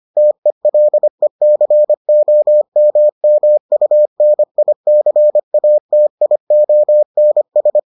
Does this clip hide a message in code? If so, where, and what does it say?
Yes: Morse code from 0.2 to 7.8 s, heard as a keyed tone, about 25 wpm, 600 Hz. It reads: TELECOMMUNICATIONS